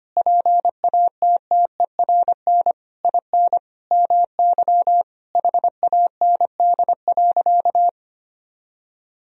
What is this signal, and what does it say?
Morse code, heard as a keyed tone, about 25 wpm, 700 Hz. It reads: PATTERN IN MY HAND.
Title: Sentences from Top 1000 Words - Review Part 2 - 25wpm